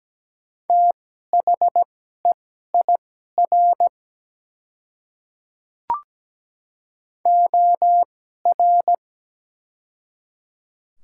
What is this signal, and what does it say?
Morse code, heard as a keyed tone, about 17 wpm, spaced as for 12 wpm, 700 Hz. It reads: THEIR OR
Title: 2x Character Spacing - Top 200 Words - 17wpm